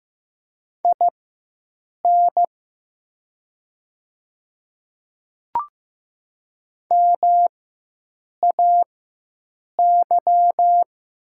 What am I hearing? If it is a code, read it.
Morse code, heard as a keyed tone, about 15 wpm, spaced as for 7 wpm, 700 Hz. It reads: IN MAY
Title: 4x Character Spacing - Top 100 Words - 15wpm